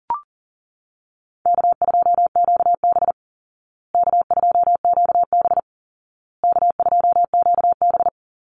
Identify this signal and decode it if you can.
Morse code, heard as a keyed tone, about 40 wpm, 700 Hz. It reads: K2QB K2QB K2QB
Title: Easy Call Signs Repeated 3x - 40wpm